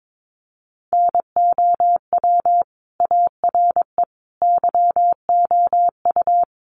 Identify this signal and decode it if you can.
Morse code, heard as a keyed tone, about 22 wpm, 700 Hz. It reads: NOW ARE YOU
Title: Sentences from Top 900 Words 22wpm